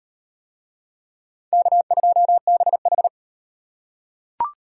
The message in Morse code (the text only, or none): K2BH